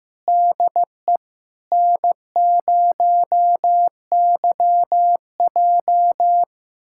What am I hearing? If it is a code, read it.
Morse code, heard as a keyed tone, about 15 wpm, 700 Hz. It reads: DE N0YJ
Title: USA Callsign DE Callsign - 4 Characters - 15wpm